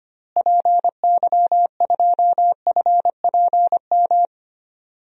Morse code, heard as a keyed tone, about 25 wpm, 700 Hz. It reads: PY2FPM
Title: DXCC-100 Call Signs - 6 Characters - Repeated 2x - 25wpm